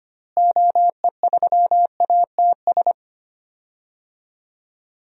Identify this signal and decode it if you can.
Morse code, heard as a keyed tone, about 25 wpm, 700 Hz. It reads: OE3ATS
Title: DXCC-100 Call Signs - 6 Characters - Repeated 3x - 25wpm